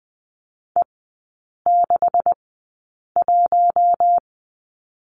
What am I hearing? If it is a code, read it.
Morse code, heard as a keyed tone, about 20 wpm, spaced as for 8 wpm, 700 Hz. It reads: E61
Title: 127 Lesson 21e - Number 6 - Three Characters 20wpm